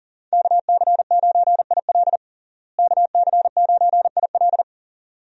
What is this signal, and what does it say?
Morse code, heard as a keyed tone, about 40 wpm, 700 Hz. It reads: KC9IL KC9IL
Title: USA Call Signs - 5 Characters - Repeated 2x - 40wpm